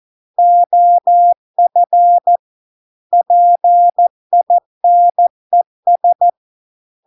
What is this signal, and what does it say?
Morse code, heard as a keyed tone, about 14 wpm, 700 Hz. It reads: OF PINES